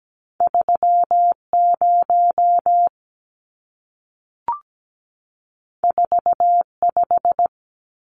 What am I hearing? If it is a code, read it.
Morse code, heard as a keyed tone, about 17 wpm, 700 Hz. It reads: 30 45
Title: Numbers - 2-Digit 17wpm